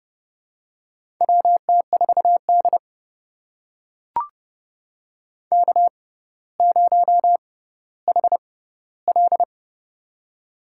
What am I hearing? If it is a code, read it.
Morse code, heard as a keyed tone, about 32 wpm, spaced as for 16 wpm, 700 Hz. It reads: WT4D K0HL
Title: USA Call Signs - 4 Characters - 6x Character Spacing - 30wpm